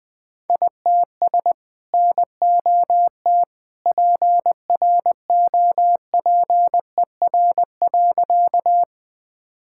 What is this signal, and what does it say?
Morse code, heard as a keyed tone, about 21 wpm, 700 Hz. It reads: ITS NOT PROPER.
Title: Sentences from Top 1000 Words - Review Part 5 - 20wpm